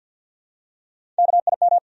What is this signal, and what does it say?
Morse code, heard as a keyed tone, about 53 wpm, 700 Hz. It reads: KIM
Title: USA Common Names - Max 3 Letters - 50wpm